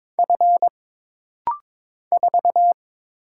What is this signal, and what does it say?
Morse code, heard as a keyed tone, about 22 wpm, 700 Hz. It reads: F 4
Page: This rapid-fire timing is geared to push your ICR (Instant Character Recognition) skills.